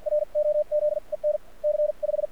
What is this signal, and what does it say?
Morse code, heard as a keyed tone, about 53 wpm, spaced as for 46 wpm, 600 Hz, with radio noise. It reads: R OZ EN C 5
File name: archived / CRSnet